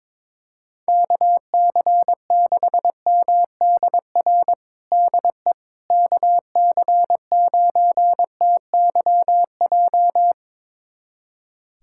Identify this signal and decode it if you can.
Morse code, heard as a keyed tone, about 22 wpm, 700 Hz. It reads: KC6MDR DE KC9TYJ
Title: USA Callsign DE Callsign - 6 Characters - 22wpm